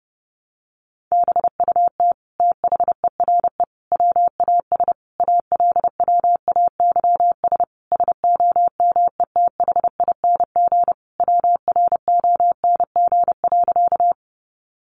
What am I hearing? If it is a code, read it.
Morse code, heard as a keyed tone, about 30 wpm, 700 Hz. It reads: BUT THERE WAS ALWAYS SOMETHING WRONG.
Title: Sentences from Top 900 Words 30wpm